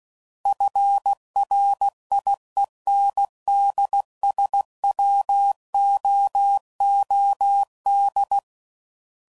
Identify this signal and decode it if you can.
Morse code, heard as a keyed tone, about 16 wpm, 800 Hz. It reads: FRIENDSWOOD
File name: CW Code Groups @15WPM / fromSM's / Cities